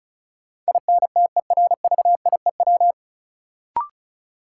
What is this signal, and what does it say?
Morse code, heard as a keyed tone, about 34 wpm, 700 Hz. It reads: INTERVIEW
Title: Sentences from Top 1600 Words - New Word Drill - 35wpm